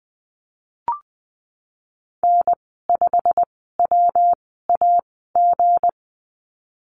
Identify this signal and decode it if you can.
Morse code, heard as a keyed tone, about 20 wpm, spaced as for 15 wpm, 700 Hz. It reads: N5WAG
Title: 186 Lesson 30g - Letter b - Call Signs 20wpm